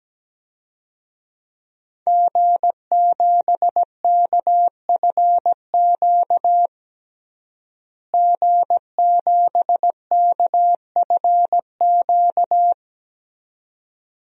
Morse code, heard as a keyed tone, about 17 wpm, 700 Hz. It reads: G7KFQ G7KFQ
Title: DXCC-100 Call Signs - 5 Characters - Repeated 2x - 17wpm